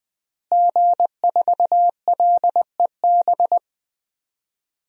Morse code, heard as a keyed tone, about 20 wpm, 700 Hz. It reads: G4LEB
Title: DXCC-100 Call Signs - 5 Characters - Repeated 3x - 20wpm